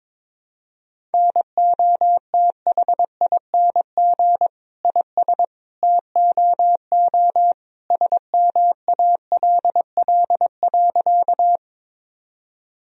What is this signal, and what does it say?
Morse code, heard as a keyed tone, about 22 wpm, 700 Hz. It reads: NOTHING IS TOO SMALL.
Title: Sentences from Top 1000 Words - Review Part 1 - 22wpm